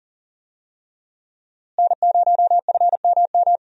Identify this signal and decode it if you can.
Morse code, heard as a keyed tone, about 37 wpm, 700 Hz. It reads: N0FMM
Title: USA Call Signs - 5 Characters - Repeated 3x - 40wpm